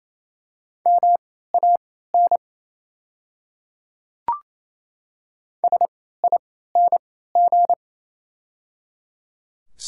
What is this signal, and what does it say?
Morse code, heard as a keyed tone, about 28 wpm, spaced as for 16 wpm, 700 Hz. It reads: MAN SING
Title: 3x Character Spacing - Top 500 Words - 28wpm